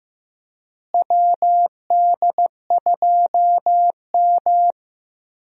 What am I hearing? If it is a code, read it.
Morse code, heard as a keyed tone, about 16 wpm, 700 Hz. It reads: WD2M
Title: USA Call Signs - 4 Characters - Repeated 3x - 15wpm